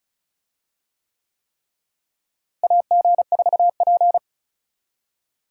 Morse code, heard as a keyed tone, about 36 wpm, 700 Hz. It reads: AG4P
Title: USA Call Signs - 4 Characters - Repeated 3x - 35wpm